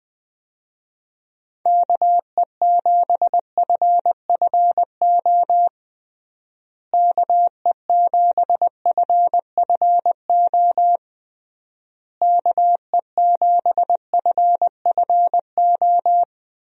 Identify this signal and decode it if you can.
Morse code, heard as a keyed tone, about 20 wpm, 700 Hz. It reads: KE7FFO KE7FFO KE7FFO
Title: USA Call Signs - 6 Characters - Repeated 3x - 20wpm